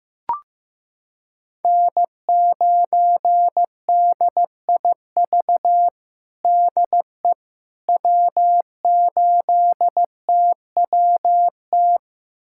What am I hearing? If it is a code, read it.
Morse code, heard as a keyed tone, about 15 wpm, 700 Hz. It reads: N9DIV DE W8TWT
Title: USA Callsign DE Callsign - 5 Characters - 15wpm